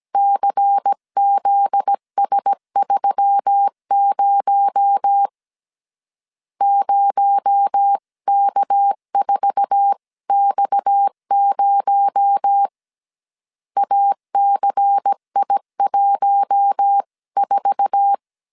Morse code, heard as a keyed tone, about 17 wpm, 800 Hz. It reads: CZS30 0K4X0 ACI14
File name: CW / DET / 1 9 1500C S A 16wpm 2828